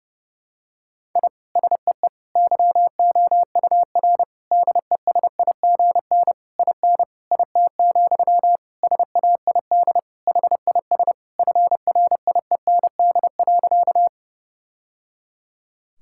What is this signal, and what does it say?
Morse code, heard as a keyed tone, about 32 wpm, 700 Hz. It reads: I SEE YOUR DESIGN IN IT, SAID HIS FRIEND.
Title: Sentences from Top 1000 Words - Review Part 5 - 30wpm